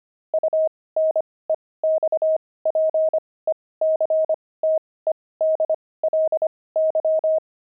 Morse code, heard as a keyed tone, about 25 wpm, spaced as for 18 wpm, 630 Hz. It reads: UNEXPECTEDLY
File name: pangram-fs-18-cs-25-aardvark